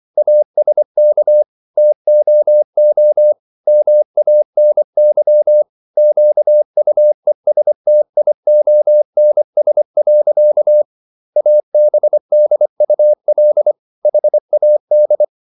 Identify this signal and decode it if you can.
Morse code, heard as a keyed tone, about 24 wpm, 600 Hz. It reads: ASK TOO MANY QUESTIONS. ABDUL HAD